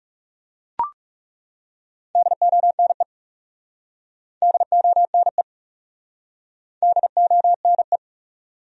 Morse code, heard as a keyed tone, about 37 wpm, 700 Hz. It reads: DONE DONE DONE
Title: Speed-Racing Top 500 Words 30wpm